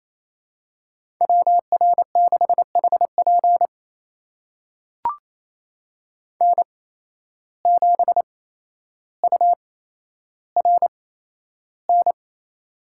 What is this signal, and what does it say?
Morse code, heard as a keyed tone, about 28 wpm, spaced as for 12 wpm, 700 Hz. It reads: WR6HP N7URN
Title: USA Call Signs - 5 Characters - 8x Character Spacing - 28wpm